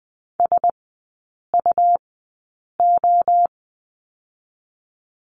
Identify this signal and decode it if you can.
Morse code, heard as a keyed tone, about 20 wpm, 700 Hz. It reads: S U O
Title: 100 Lesson 17e - Letter u - Three Characters 20wpm